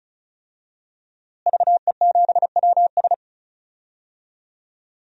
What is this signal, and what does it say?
Morse code, heard as a keyed tone, about 36 wpm, 700 Hz. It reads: VE7WS
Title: DXCC-100 Call Signs - 5 Characters - Repeated 2x - 35wpm